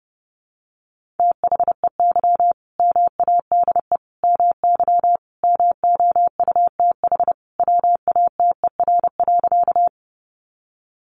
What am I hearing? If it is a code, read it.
Morse code, heard as a keyed tone, about 30 wpm, 700 Hz. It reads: THEY MADE MY MOUTH WATER.
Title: Sentences from Top 1500 Words 30wpm